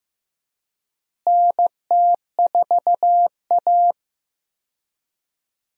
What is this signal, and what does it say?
Morse code, heard as a keyed tone, about 16 wpm, 700 Hz. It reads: NT4A
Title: USA Call Signs - 4 Characters - Repeated 2x - 15wpm